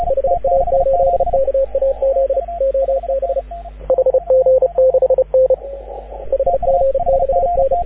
Morse code, heard as a keyed tone, about 36 wpm, 530 Hz, with radio noise. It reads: HA8KAZ OB HG6N HG6N
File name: AUDIO2019 / HG6N CW SSB MIX